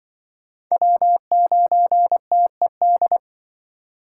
Morse code, heard as a keyed tone, about 24 wpm, 700 Hz. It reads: W9TED
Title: Speed Racing - USA 1x3 Call Signs 20wpm